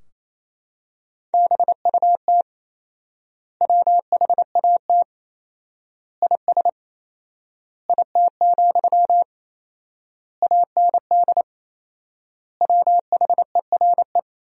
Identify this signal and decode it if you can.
Morse code, heard as a keyed tone, about 28 wpm, 700 Hz. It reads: BUT WHAT IS IT, AND WHERE
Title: 4x Word-Spacing - Sentences from Top 300 Words - 28wpm